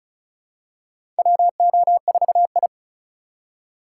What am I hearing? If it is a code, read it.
Morse code, heard as a keyed tone, about 34 wpm, 700 Hz. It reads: WO4I
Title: USA Call Signs - 4 Characters - 2x Character Spacing - 35wpm